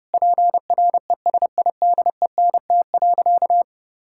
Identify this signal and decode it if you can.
Morse code, heard as a keyed tone, about 32 wpm, 700 Hz. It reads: PRESIDENT.
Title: Sentences from Top 1600 Words 30wpm